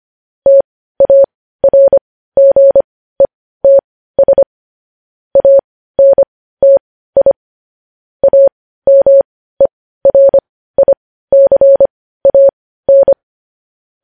Morse code, heard as a keyed tone, about 25 wpm, spaced as for 15 wpm, 550 Hz. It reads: TARGETS ANTI AMERICAN
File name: podcast / News headlines-1x-25x15